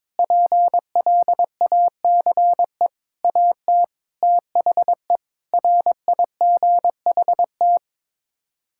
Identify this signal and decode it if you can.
Morse code, heard as a keyed tone, about 22 wpm, 700 Hz. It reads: PLACE AT THE RIGHT
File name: practice / 22wpm / Sets of 6 Words - 22wpm